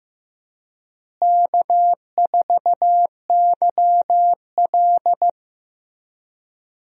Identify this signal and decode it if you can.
Morse code, heard as a keyed tone, about 15 wpm, 700 Hz. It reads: K4YL
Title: USA Call Signs - 4 Characters - Repeated 2x - 15wpm